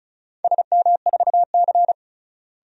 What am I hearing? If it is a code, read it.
Morse code, heard as a keyed tone, about 34 wpm, 700 Hz. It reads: SM4C
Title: DXCC-100 Call Signs - 4 Characters - Repeated 2x - 35wpm